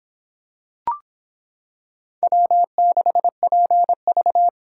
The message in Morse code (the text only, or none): W6PV